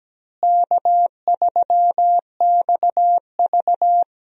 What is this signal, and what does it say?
Morse code, heard as a keyed tone, about 17 wpm, 700 Hz. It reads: K3XV
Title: USA Call Signs - 4 Characters - Repeated 2x - 17wpm